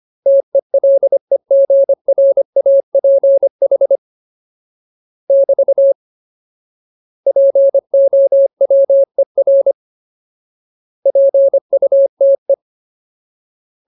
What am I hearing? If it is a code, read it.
Morse code, heard as a keyed tone, about 25 wpm, 550 Hz. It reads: TELEGRAPH <BT> POWER PUTE
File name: Ultimate Word List / 25WPM Ultimate